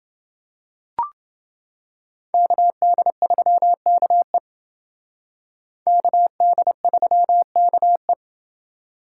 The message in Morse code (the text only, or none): KD3KE KD3KE